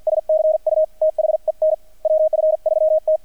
Morse code, heard as a keyed tone, about 48 wpm, 630 Hz, with radio noise. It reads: S CU THEN WF3T
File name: archived / CRSnet